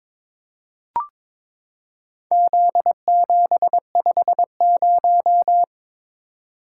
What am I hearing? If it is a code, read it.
Morse code, heard as a keyed tone, about 21 wpm, 700 Hz. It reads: Z750
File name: practice / 22wpm / Four Letter-Number Combinations - 22wpm